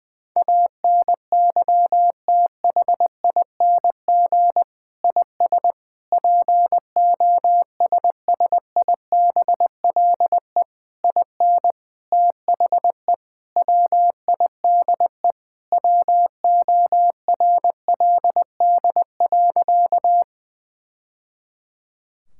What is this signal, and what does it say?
Morse code, heard as a keyed tone, about 19 wpm, 700 Hz. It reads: ANYTHING IS POSSIBLE IN THE WIDE WORLD.
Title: Sentences from Top 1000 Words - Review Part 6 - 20wpm